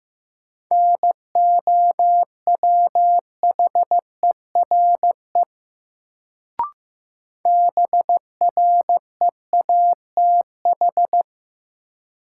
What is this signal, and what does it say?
Morse code, heard as a keyed tone, about 15 wpm, 700 Hz. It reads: NOWHERE BREATH
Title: Sets of 5 Words Encore - New Word Drill - 15wpm